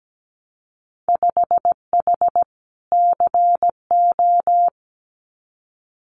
Morse code, heard as a keyed tone, about 17 wpm, 700 Hz. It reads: 5H CO
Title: Winter Field Day Exchanges Encore 17wpm